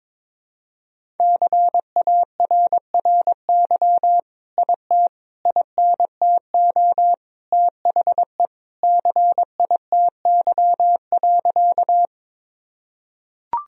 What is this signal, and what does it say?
Morse code, heard as a keyed tone, about 22 wpm, 700 Hz. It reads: CARRY IT INTO THE CITY.
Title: Sentences from Top 1000 Words - Review Part 1 - 22wpm